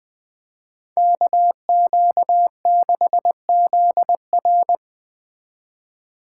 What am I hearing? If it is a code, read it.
Morse code, heard as a keyed tone, about 20 wpm, 700 Hz. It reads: KQ6ZR
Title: Speed Racing - USA Call Signs 17wpm